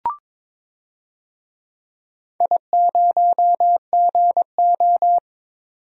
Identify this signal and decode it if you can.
Morse code, heard as a keyed tone, about 22 wpm, 700 Hz. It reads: I0GO
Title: DXCC-100 Call Signs - 4 Characters - 22wpm